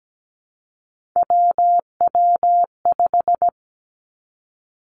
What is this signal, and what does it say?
Morse code, heard as a keyed tone, about 17 wpm, 700 Hz. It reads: WW5
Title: Three Challenging Letter-Number Combinations 17wpm